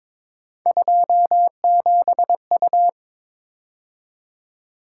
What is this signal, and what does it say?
Morse code, heard as a keyed tone, about 27 wpm, 700 Hz. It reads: 27U
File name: Speed-Racing - Three Letter-Number Combinations - 22wpm